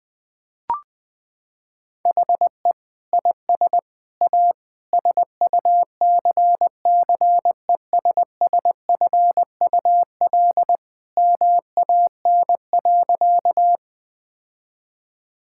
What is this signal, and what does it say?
Morse code, heard as a keyed tone, about 21 wpm, 700 Hz. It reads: HE IS A SUCCESSFUL MAN.
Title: Sentences from Top 1000 Words - Review Part 6 - 20wpm